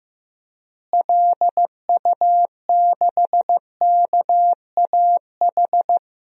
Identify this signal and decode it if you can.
Morse code, heard as a keyed tone, about 15 wpm, 700 Hz. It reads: LU6KAH
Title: DXCC-100 Call Signs - 6 Characters - Repeated 2x - 15wpm